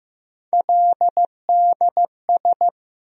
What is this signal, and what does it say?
Morse code, heard as a keyed tone, about 15 wpm, 700 Hz. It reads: LDS
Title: USA Call Sign Postfixes - 3 Letters 15wpm